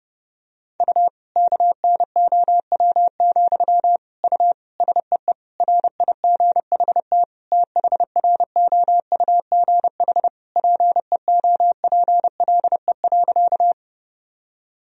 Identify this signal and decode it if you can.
Morse code, heard as a keyed tone, about 32 wpm, 700 Hz. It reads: U KNOW, U SEE RIGHT THROUGH PEOPLE.
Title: Sentences from Top 200 Words and Abbreviations 30wpm